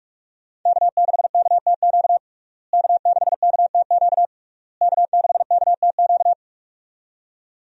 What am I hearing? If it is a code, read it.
Morse code, heard as a keyed tone, about 46 wpm, 700 Hz. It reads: K6KTQ K6KTQ K6KTQ
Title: USA Call Signs - 5 Characters - Repeated 3x - 45wpm